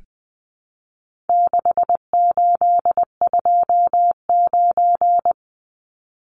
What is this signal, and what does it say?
Morse code, heard as a keyed tone, about 20 wpm, 700 Hz. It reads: 6829
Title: Numbers - 4-Digit 20wpm